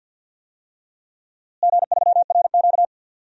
Morse code, heard as a keyed tone, about 50 wpm, 700 Hz. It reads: G3RQ
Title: DXCC-100 Call Signs - 4 Characters - Repeated 3x - 50wpm